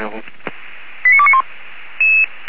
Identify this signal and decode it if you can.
Five-tone selective call (ZVEI), heard as quick runs of five tones, tones of about 70 ms, with speech. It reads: 89291